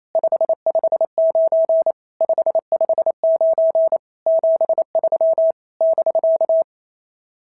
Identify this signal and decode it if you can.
Morse code, heard as a keyed tone, about 27 wpm, 650 Hz. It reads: 559 559 73 <BK>
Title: SOTA QSO Practice 28wpm